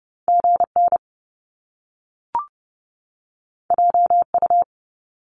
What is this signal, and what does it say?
Morse code, heard as a keyed tone, about 30 wpm, 700 Hz. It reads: G N J U
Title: Top Two Letter Combinations in Words 30wpm